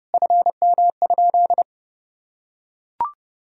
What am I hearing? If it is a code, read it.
Morse code, heard as a keyed tone, about 30 wpm, 700 Hz. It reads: FM?
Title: Sentences from Top 200 Words and Abbreviations 30wpm